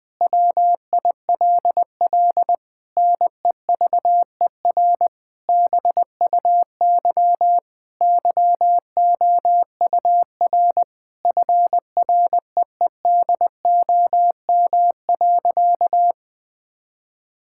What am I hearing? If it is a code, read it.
Morse code, heard as a keyed tone, about 20 wpm, 700 Hz. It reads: WILL NEVER BUY YOUR FREEDOM.
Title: Sentences from Top 1000 Words - Review Part 5 - 20wpm